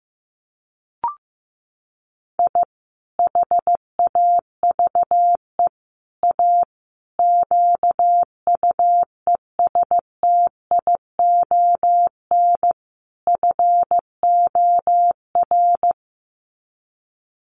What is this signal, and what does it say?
Morse code, heard as a keyed tone, about 15 wpm, 700 Hz. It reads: I HAVE A QUESTION FOR
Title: Sets of 5 Words Encore2 15wpm